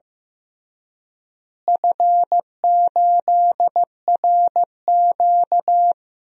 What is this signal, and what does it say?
Morse code, heard as a keyed tone, about 15 wpm, 700 Hz. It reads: F8RQ
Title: DXCC-100 Call Signs - 4 Characters - Repeated 3x - 15wpm